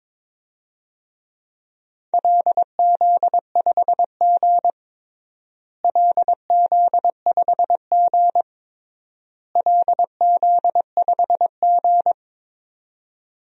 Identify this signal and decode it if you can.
Morse code, heard as a keyed tone, about 22 wpm, 700 Hz. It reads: LZ5G LZ5G LZ5G
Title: DXCC-100 Call Signs - 4 Characters - Repeated 3x - 22wpm